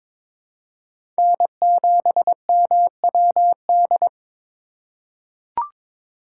Morse code, heard as a keyed tone, about 21 wpm, 700 Hz. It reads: N7MWD
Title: USA Call Signs - 5 Characters - 4x Character Spacing - 22wpm